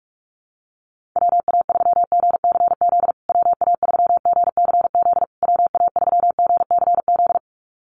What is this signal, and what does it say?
Morse code, heard as a keyed tone, about 44 wpm, 700 Hz. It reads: WA3GCZ WA3GCZ WA3GCZ
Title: USA Call Signs Repeated 3x 45wpm